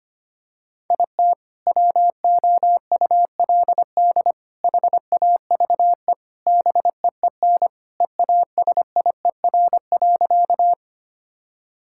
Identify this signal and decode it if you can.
Morse code, heard as a keyed tone, about 25 wpm, 700 Hz. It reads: IT WOULD HAVE BEEN EASIER.
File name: practice / 25wpm / Sentences with 3-Word Phrases - 25wpm